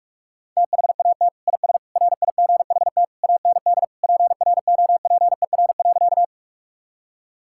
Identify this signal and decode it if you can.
Morse code, heard as a keyed tone, about 41 wpm, 700 Hz. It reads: THAT IS RIGHT AND PROPER.
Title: Sentences from Top 1000 Words - Review Part 5 - 45wpm